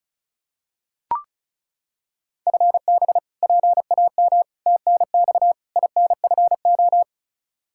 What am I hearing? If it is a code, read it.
Morse code, heard as a keyed tone, about 36 wpm, 700 Hz. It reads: FB PAM TNX INFO
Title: K1USN SST QSO Practice 35wpm